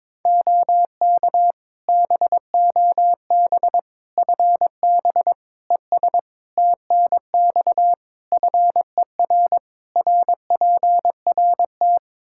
Morse code, heard as a keyed tone, about 22 wpm, 700 Hz. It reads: OK BOB FB ES TNX FER RPRT
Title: Practice QSO - Exchange 2 - Info - 22wpm